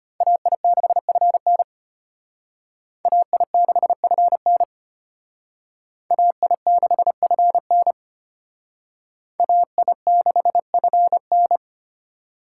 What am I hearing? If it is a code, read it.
Morse code, heard as a keyed tone, about 32 wpm, 700 Hz. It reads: AI6FN AI6FN AI6FN AI6FN